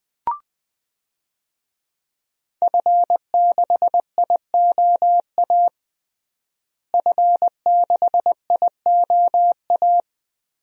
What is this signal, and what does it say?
Morse code, heard as a keyed tone, about 20 wpm, 700 Hz. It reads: F6IOA F6IOA
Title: DXCC-100 Call Signs - 5 Characters - Repeated 3x - 20wpm